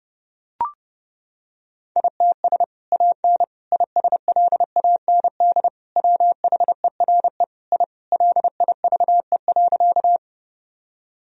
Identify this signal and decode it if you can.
Morse code, heard as a keyed tone, about 32 wpm, 700 Hz. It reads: ITS AN ISLAND WHERE I LIVE.
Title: Sentences from Top 1000 Words - Review Part 3 - 30wpm